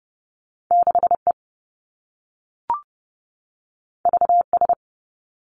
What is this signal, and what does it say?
Morse code, heard as a keyed tone, about 30 wpm, 700 Hz. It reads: 6 E V S
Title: Two Challenging Letter-Number Combinations 30wpm